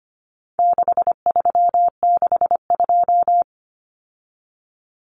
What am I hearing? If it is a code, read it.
Morse code, heard as a keyed tone, about 25 wpm, 700 Hz. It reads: 6362
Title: Numbers - 4-Digit 25wpm